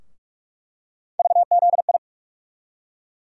Morse code, heard as a keyed tone, about 47 wpm, 700 Hz. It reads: VZI
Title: USA Call Sign Postfixes - 3 Letters 45wpm